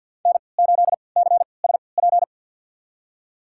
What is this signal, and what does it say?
Morse code, heard as a keyed tone, about 47 wpm, spaced as for 28 wpm, 700 Hz. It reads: N8CSP
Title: USA Call Signs - 3x Character Spacing - 50wpm